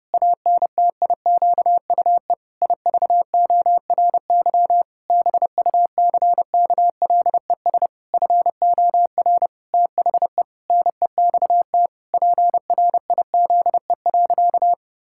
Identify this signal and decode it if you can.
Morse code, heard as a keyed tone, about 30 wpm, 700 Hz. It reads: ANTIQUE IVORY BUCKLES FOR THE NEXT PRIZE.
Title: Pangram Challenge v2 - 30wpm